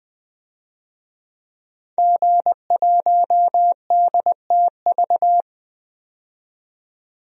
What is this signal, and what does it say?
Morse code, heard as a keyed tone, about 20 wpm, 700 Hz. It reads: G1DTV
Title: DXCC-100 Call Signs - 5 Characters - 20wpm